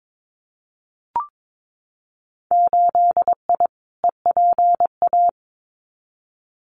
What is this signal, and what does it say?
Morse code, heard as a keyed tone, about 22 wpm, 700 Hz. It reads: 8I EPA
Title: Winter Field Day Exchanges Encore 22wpm